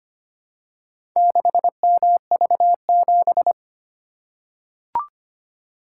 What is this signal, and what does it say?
Morse code, heard as a keyed tone, about 25 wpm, 700 Hz. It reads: 6MV7
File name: practice / 25wpm / Four Letter-Number Combinations - 25wpm